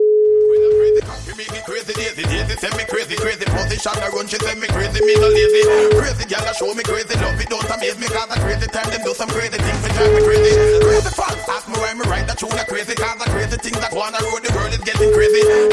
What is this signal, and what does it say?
Ringback tone (Europe, 425 Hz), with noise.